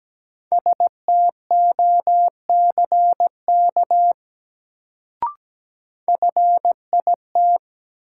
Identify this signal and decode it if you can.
Morse code, heard as a keyed tone, about 17 wpm, 700 Hz. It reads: STOCK FIT
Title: Sentences from Top 800 Words - New Word Drill - 17wpm